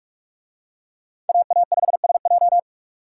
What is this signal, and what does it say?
Morse code, heard as a keyed tone, about 45 wpm, 700 Hz. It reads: AA5SJ